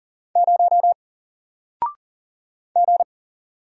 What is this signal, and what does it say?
Morse code, heard as a keyed tone, about 40 wpm, 700 Hz. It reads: 0 G
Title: Single Letter-Number Alt 40wpm